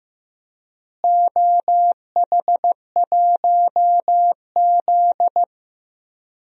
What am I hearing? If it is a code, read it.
Morse code, heard as a keyed tone, about 15 wpm, 700 Hz. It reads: OH1Z
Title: DXCC-100 Call Signs - 4 Characters - Repeated 2x - 15wpm